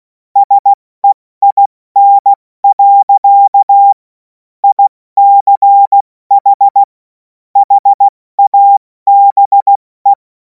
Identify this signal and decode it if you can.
Morse code, heard as a keyed tone, about 16 wpm, 800 Hz. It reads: SEIN. ICH HABE
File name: Morsekurs / Witze / Witz 10. Februar 2026 70 F80